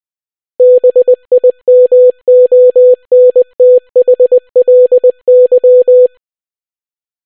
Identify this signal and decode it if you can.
Morse code, heard as a keyed tone, about 20 wpm, 500 Hz. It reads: BIMONTHLY